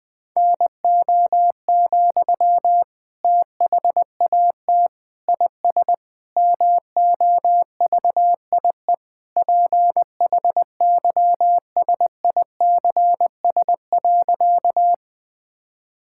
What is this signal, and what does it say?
Morse code, heard as a keyed tone, about 20 wpm, 700 Hz. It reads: NO, THAT IS MOVIE PHYSICS.
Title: Sentences from Top 1000 Words - Review Part 5 - 20wpm